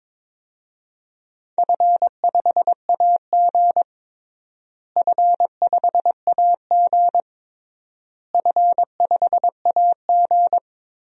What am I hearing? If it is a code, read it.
Morse code, heard as a keyed tone, about 21 wpm, 700 Hz. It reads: F5AG F5AG F5AG
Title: DXCC-100 Call Signs - 4 Characters - Repeated 3x - 22wpm